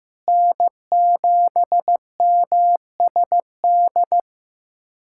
Morse code, heard as a keyed tone, about 15 wpm, 700 Hz. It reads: N7MSD
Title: USA Callsign DE Callsign 15wpm